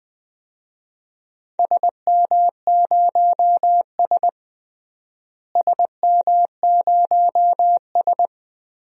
Morse code, heard as a keyed tone, about 21 wpm, 700 Hz. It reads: SM0S SM0S
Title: DXCC-100 Call Signs - 4 Characters - Repeated 2x - 20wpm